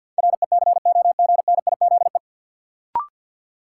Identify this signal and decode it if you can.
Morse code, heard as a keyed tone, about 52 wpm, 700 Hz. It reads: RECOGNIZE
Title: Sentences from Top 1400 Words - New Word Drill - 50wpm